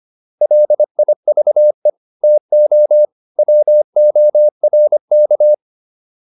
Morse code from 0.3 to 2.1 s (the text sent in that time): LIVE